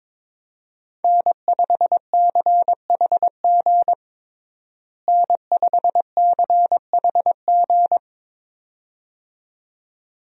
Morse code, heard as a keyed tone, about 22 wpm, 700 Hz. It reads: N5CHG N5CHG
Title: USA Call Signs - 5 Characters - Repeated 2x - 22wpm